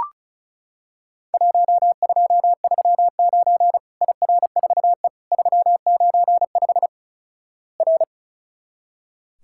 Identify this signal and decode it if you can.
Morse code, heard as a keyed tone, about 33 wpm, 700 Hz. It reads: 1239 IR4E 395 R
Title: Worked All Europe DX Contest - QTC Exchange - 35wpm